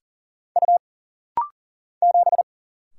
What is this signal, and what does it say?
Morse code, heard as a keyed tone, about 40 wpm, 700 Hz. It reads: U 7
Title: Single Letter-Number Alt - Rapid-Fire 40wpm